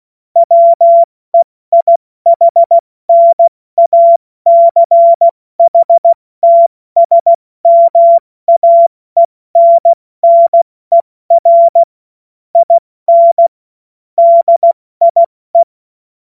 Morse code, heard as a keyed tone, about 16 wpm, 680 Hz. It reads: WEIHNACHTSMAENNER IN DIE
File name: Witz-14-Oktober-2025-70-F80